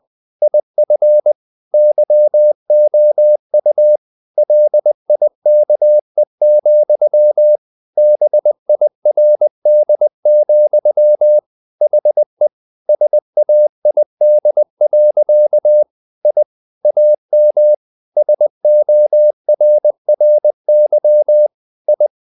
Morse code, heard as a keyed tone, about 20 wpm, 600 Hz. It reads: IF YOU LIKE, BIRD, HE SAID. I AM SORRY I
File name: the old man and the sea / the old man and the sea part 1 20wpm